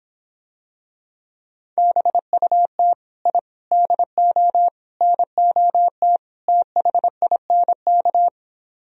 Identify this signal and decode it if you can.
Morse code, heard as a keyed tone, about 26 wpm, 700 Hz. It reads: BUT I DO NOT THINK